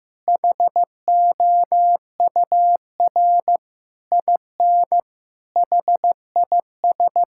Morse code, heard as a keyed tone, about 15 wpm, 700 Hz. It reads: HOUR IN HIS
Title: Sentences from Top 1000 Words - Review Part 6 - 15wpm